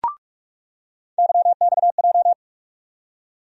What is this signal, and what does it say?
Morse code, heard as a keyed tone, about 46 wpm, 700 Hz. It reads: YXJ
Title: USA Call Sign Postfixes - 3 Letters 45wpm